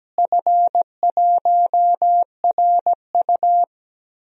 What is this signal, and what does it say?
Morse code, heard as a keyed tone, about 17 wpm, 700 Hz. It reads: F1RU
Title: DXCC-100 Call Signs - 4 Characters - Repeated 3x - 17wpm